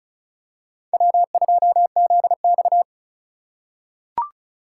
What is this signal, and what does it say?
Morse code, heard as a keyed tone, about 34 wpm, 700 Hz. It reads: W2ZX
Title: USA Call Signs - 4 Characters - 4x Character Spacing - 35wpm